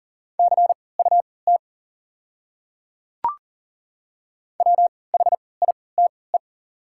Morse code, heard as a keyed tone, about 40 wpm, spaced as for 23 wpm, 700 Hz. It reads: CUT WHITE
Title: 3x Character Spacing - Top 300 Words - 40wpm